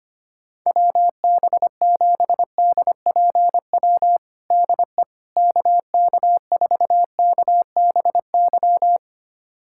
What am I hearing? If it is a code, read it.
Morse code, heard as a keyed tone, about 25 wpm, 700 Hz. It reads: WB7DPW DE KK4KBY
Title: USA Callsign DE Callsign - 6 Characters - 25wpm